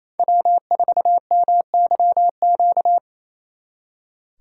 Morse code, heard as a keyed tone, about 35 wpm, 700 Hz. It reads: W4MYQ
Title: Speed Racing - USA 1x3 Call Signs 28wpm